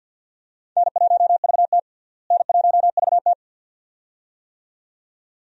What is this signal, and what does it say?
Morse code, heard as a keyed tone, about 46 wpm, 700 Hz. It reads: N1VT N1VT
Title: USA Call Signs - 4 Characters - Repeated 2x - 50wpm